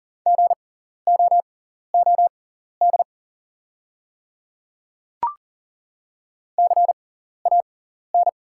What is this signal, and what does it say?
Morse code, heard as a keyed tone, about 40 wpm, spaced as for 14 wpm, 690 Hz. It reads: GOOD CAN
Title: 6x Character Spacing - Top 300 Words - 40wpm